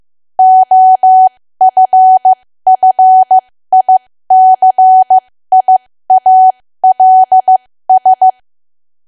Morse code, heard as a keyed tone, about 15 wpm, spaced as for 13 wpm, 750 Hz. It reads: OFFICIALS
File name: cwa / W1AW-Level2 / 13WPM